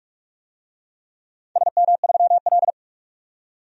Morse code, heard as a keyed tone, about 43 wpm, 700 Hz. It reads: SM3L
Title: DXCC-100 Call Signs - 4 Characters - 45wpm